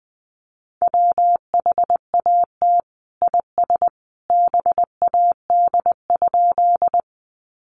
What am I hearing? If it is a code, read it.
Morse code, heard as a keyed tone, about 20 wpm, 700 Hz. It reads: WHAT IS BAD?
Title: Sentences from Top 700 Words 20wpm